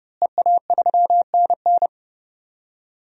Morse code, heard as a keyed tone, about 32 wpm, 700 Hz. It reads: EA3NN
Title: DXCC-100 Call Signs - 5 Characters - Repeated 2x - 30wpm